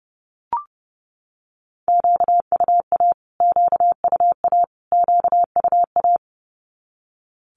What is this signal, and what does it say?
Morse code, heard as a keyed tone, about 30 wpm, 700 Hz. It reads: QUA QUA QUA
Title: Q-Codes Repeated 3x 30wpm